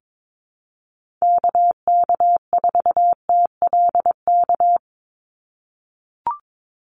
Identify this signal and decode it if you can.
Morse code, heard as a keyed tone, about 22 wpm, 700 Hz. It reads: KK4TLK
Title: USA Call Signs 22wpm